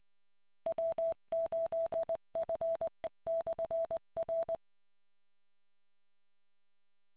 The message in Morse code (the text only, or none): W8F/R